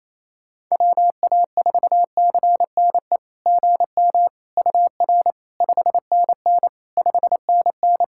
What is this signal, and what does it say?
Morse code, heard as a keyed tone, about 28 wpm, 700 Hz. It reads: WA4CNE GM UR 5NN 5NN
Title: SOTA QSO Practice 28wpm